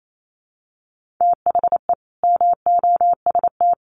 Morse code, heard as a keyed tone, about 28 wpm, 700 Hz. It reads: THE MOST
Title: Sets of 4 Words 28wpm